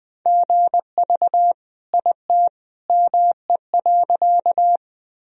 Morse code, heard as a keyed tone, about 20 wpm, 700 Hz. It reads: GV IT ME.
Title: Sentences from Top 200 Words and Abbreviations 20wpm